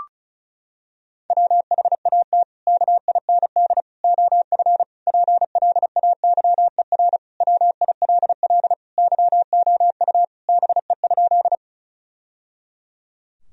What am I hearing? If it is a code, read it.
Morse code, heard as a keyed tone, about 36 wpm, 700 Hz. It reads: WHAT KIND OF PLAYER WILL YOU BE?
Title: Sentences from Top 1000 Words - Review Part 4 - 35wpm